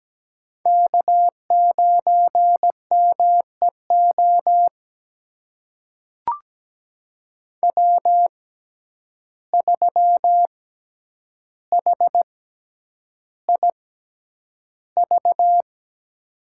Morse code, heard as a keyed tone, about 17 wpm, spaced as for 9 wpm, 700 Hz. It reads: K9MEO W3HIV
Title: USA Call Signs - 5 Characters - 6x Character Spacing - 17wpm